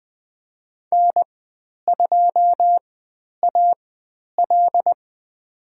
Morse code, heard as a keyed tone, about 20 wpm, spaced as for 10 wpm, 700 Hz. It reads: N2AL
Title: 102 Lesson 17g - Letter u - Call Signs 20wpm